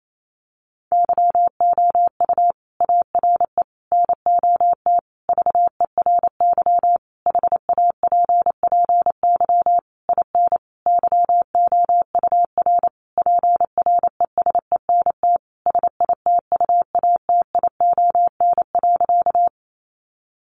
Morse code, heard as a keyed tone, about 28 wpm, 700 Hz. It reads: YOU ARE NOT VERY HAPPY IN YOUR PRESENT SITUATION.
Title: Sentences from Top 1200 Words 28wpm